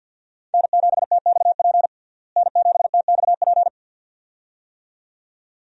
Morse code, heard as a keyed tone, about 49 wpm, 700 Hz. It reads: N7TXP N7TXP
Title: USA Call Signs - 5 Characters - Repeated 2x - 50wpm